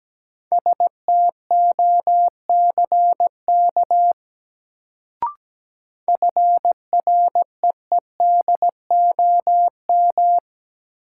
Morse code, heard as a keyed tone, about 17 wpm, 700 Hz. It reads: STOCK FREEDOM
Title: Sentences from Top 800 Words - New Word Drill - 17wpm